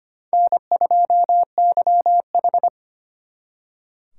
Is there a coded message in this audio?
Yes: Morse code from 0.3 to 2.7 s, heard as a keyed tone, about 33 wpm, 700 Hz. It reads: N2YH